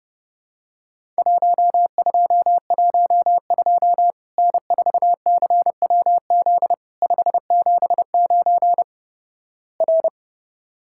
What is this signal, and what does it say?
Morse code, heard as a keyed tone, about 30 wpm, 700 Hz. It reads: 1212 N4CWZ 579 R
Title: Worked All Europe DX Contest - QTC Exchange - 30wpm